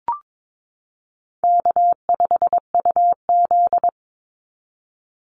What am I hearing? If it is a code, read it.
Morse code, heard as a keyed tone, about 22 wpm, 700 Hz. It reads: K5UZ
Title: Easy Call Signs Repeated 2x - 22wpm